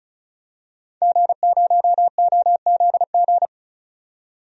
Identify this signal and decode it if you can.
Morse code, heard as a keyed tone, about 36 wpm, 700 Hz. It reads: G0OZG
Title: DXCC-100 Call Signs - 5 Characters - 35wpm